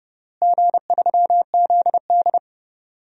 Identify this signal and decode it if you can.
Morse code, heard as a keyed tone, about 31 wpm, 700 Hz. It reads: G3ZD
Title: DXCC-100 Call Signs - 4 Characters - Repeated 3x - 30wpm